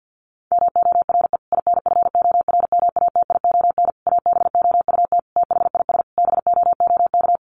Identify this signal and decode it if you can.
Morse code, heard as a keyed tone, about 50 wpm, 700 Hz. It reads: MORE INFORMATION ABOUT THIS BOOK
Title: Sentences from Top 1200 Words 50wpm